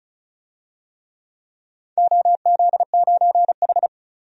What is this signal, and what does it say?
Morse code, heard as a keyed tone, about 36 wpm, 700 Hz. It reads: OZ9H
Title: DXCC-100 Call Signs - 4 Characters - 35wpm